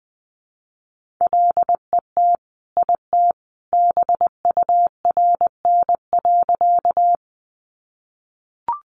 Morse code, heard as a keyed tone, about 20 wpm, 700 Hz. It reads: LET IT BURN.
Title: Sentences from Top 1300 Words 20wpm